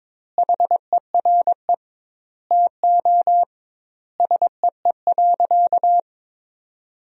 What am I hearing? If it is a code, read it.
Morse code, heard as a keyed tone, about 21 wpm, 700 Hz. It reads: HERE TO SEE.
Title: 2x Word-Spacing - Sentences from Top 200 Words - 22wpm